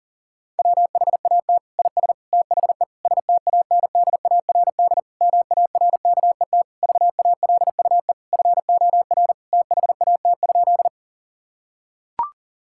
Morse code, heard as a keyed tone, about 40 wpm, 700 Hz. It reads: WHAT IS THE STANDARD MARKET VALUE FOR THAT?
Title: Sentences from Top 1000 Words - Review Part 6 - 40wpm